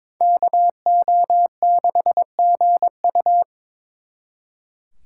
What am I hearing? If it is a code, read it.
Morse code, heard as a keyed tone, about 26 wpm, 700 Hz. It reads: KO6GU
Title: Speed Racing - USA 2x2 Call Signs 22wpm